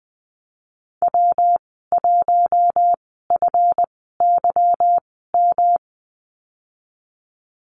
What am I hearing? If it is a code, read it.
Morse code, heard as a keyed tone, about 19 wpm, spaced as for 14 wpm, 700 Hz. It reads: W1FYM
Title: 179 Lesson 29g - Slash - Call Signs 20wpm